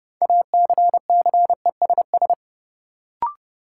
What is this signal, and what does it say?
Morse code, heard as a keyed tone, about 30 wpm, 700 Hz. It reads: ACCESS
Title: Sentences from Top 1000 Words - Review Part 4 - New Word Drill - 30wpm